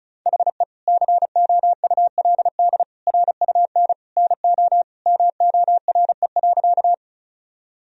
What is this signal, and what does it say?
Morse code, heard as a keyed tone, about 36 wpm, 700 Hz. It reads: HE COULD RUN NO MORE.
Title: Sentences from Top 1000 Words - Review Part 1 - 35wpm